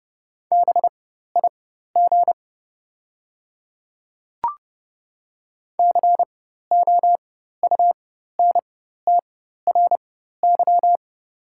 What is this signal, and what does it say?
Morse code, heard as a keyed tone, about 30 wpm, spaced as for 14 wpm, 700 Hz. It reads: BIG COUNTRY
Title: 4x Character Spacing - Top 500 Words - 30wpm